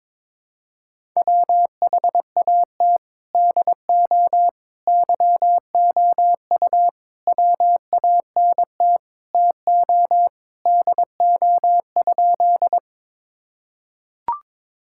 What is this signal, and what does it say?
Morse code, heard as a keyed tone, about 22 wpm, 700 Hz. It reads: WHAT DO YOU WANT TO DO?
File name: practice / 22wpm / Sentences with 3-Word Phrases - 22wpm